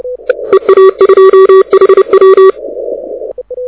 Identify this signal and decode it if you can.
Morse code, heard as a keyed tone, about 30 wpm, 380 Hz. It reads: EA2HW
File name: audio zawody / IARU 2015 CW SN7Q